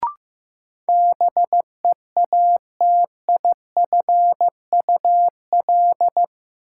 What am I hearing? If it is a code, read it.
Morse code, heard as a keyed tone, about 15 wpm, 700 Hz. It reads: BEATIFUL
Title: Sentences from Top 1300 Words - New Word Drill - 15wpm